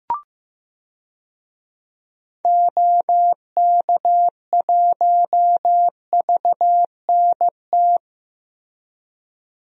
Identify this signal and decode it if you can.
Morse code, heard as a keyed tone, about 16 wpm, 700 Hz. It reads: OK1VNT
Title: DXCC-100 Call Signs - 6 Characters - Repeated 2x - 15wpm